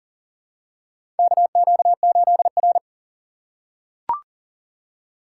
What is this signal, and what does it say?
Morse code, heard as a keyed tone, about 44 wpm, 700 Hz. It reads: KQ8R
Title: USA Call Signs - 4 Characters - 6x Character Spacing - 40wpm